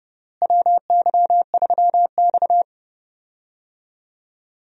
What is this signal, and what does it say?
Morse code, heard as a keyed tone, about 32 wpm, 700 Hz. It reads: WY3X
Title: USA Call Signs - 4 Characters - Repeated 2x - 30wpm